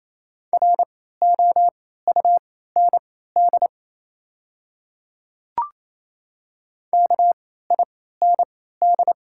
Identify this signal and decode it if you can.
Morse code, heard as a keyed tone, about 27 wpm, spaced as for 16 wpm, 700 Hz. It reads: ROUND KIND
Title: 3x Character Spacing - Top 200 Words - 28wpm